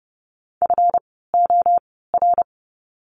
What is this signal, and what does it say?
Morse code, heard as a keyed tone, about 30 wpm, spaced as for 17 wpm, 700 Hz. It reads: FOR
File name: course / 30:10wpm / 237 Lesson 38f - Letter q - Words 30wpm